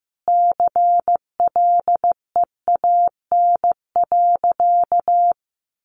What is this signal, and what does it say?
Morse code, heard as a keyed tone, about 14 wpm, 700 Hz. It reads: CLEAN.
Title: Sentences from Top 1100 Words 15wpm